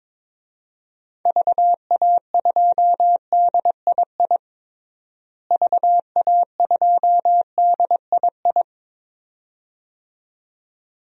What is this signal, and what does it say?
Morse code, heard as a keyed tone, about 22 wpm, 700 Hz. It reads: VA2DII VA2DII
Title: DXCC-100 Call Signs - 6 Characters - Repeated 2x - 22wpm